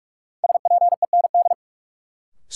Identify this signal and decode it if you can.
Morse code, heard as a keyed tone, about 46 wpm, 700 Hz. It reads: SPEND
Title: Sets of 5 Words - New Word Drill - 45wpm